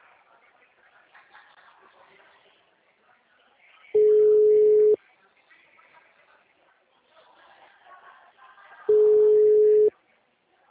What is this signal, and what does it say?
Ringback tone (Europe, 425 Hz).
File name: out- -2006- - - .72022